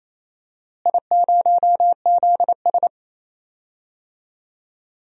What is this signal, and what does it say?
Morse code, heard as a keyed tone, about 28 wpm, 700 Hz. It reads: I0ZS
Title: DXCC-100 Call Signs - 4 Characters - Repeated 3x - 28wpm